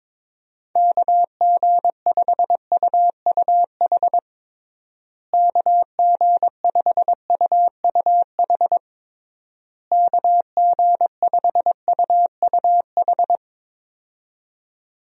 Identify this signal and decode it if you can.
Morse code, heard as a keyed tone, about 22 wpm, 700 Hz. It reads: KG5UUH KG5UUH KG5UUH
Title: USA Call Signs - 6 Characters - Repeated 3x - 22wpm